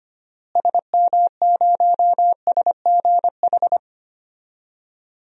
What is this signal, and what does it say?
Morse code, heard as a keyed tone, about 25 wpm, 700 Hz. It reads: SM0SGH
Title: DXCC-100 Call Signs - 6 Characters - Repeated 2x - 25wpm